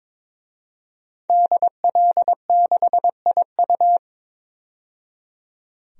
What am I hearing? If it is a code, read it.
Morse code, heard as a keyed tone, about 22 wpm, 700 Hz. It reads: DL6IU
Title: DXCC-100 Call Signs - 5 Characters - 22wpm